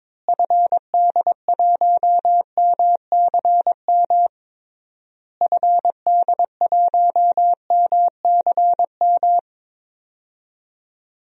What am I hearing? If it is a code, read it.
Morse code, heard as a keyed tone, about 22 wpm, 700 Hz. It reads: FD1MCM FD1MCM
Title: DXCC-100 Call Signs - 6 Characters - Repeated 3x - 22wpm